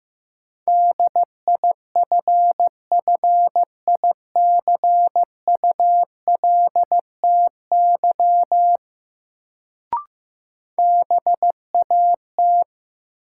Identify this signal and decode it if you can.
Morse code, heard as a keyed tone, about 15 wpm, 700 Hz. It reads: DIFFICULTY BAT
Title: Sentences from Top 1500 Words - New Word Drill - 15wpm